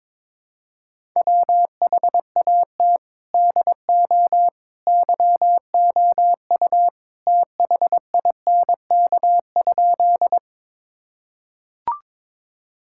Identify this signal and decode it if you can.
Morse code, heard as a keyed tone, about 22 wpm, 700 Hz. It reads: WHAT DO YOU THINK?
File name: practice / 22wpm / Sentences with 2-Word Phrases - 22wpm